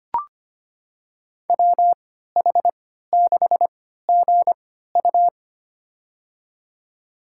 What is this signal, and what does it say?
Morse code, heard as a keyed tone, about 25 wpm, spaced as for 14 wpm, 700 Hz. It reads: WH6GU
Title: USA Call Signs - 3x Character Spacing - 25wpm